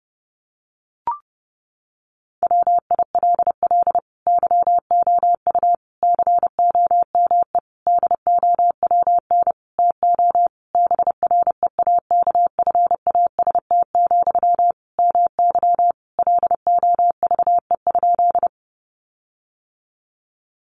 Morse code, heard as a keyed tone, about 28 wpm, 700 Hz. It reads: WILL YOU COME DOWN TO BREAKFAST, MY LOVE?
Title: Sentences from Top 1500 Words 30wpm